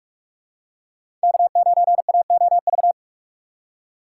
Morse code, heard as a keyed tone, about 46 wpm, 700 Hz. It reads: K9AOV
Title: USA Call Signs - 4x Character Spacing - 45wpm